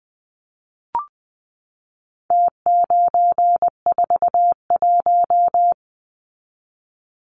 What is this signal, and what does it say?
Morse code, heard as a keyed tone, about 20 wpm, 700 Hz. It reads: T941
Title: Numbers - Contest Style - 4-Digit 20wpm